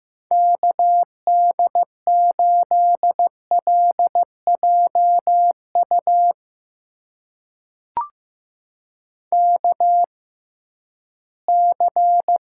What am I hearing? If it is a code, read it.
Morse code, heard as a keyed tone, about 15 wpm, spaced as for 11 wpm, 700 Hz. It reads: KD8LJU KC
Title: USA Call Signs - 6x Character Spacing - 15wpm